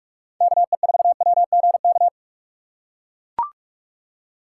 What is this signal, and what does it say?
Morse code, heard as a keyed tone, about 43 wpm, 700 Hz. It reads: KE4WGK
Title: USA Call Signs - 6 Characters - 6x Character Spacing - 45wpm